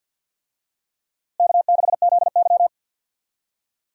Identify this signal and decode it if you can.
Morse code, heard as a keyed tone, about 50 wpm, 700 Hz. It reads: K6ZY
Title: USA Call Signs - 4 Characters - Repeated 2x - 50wpm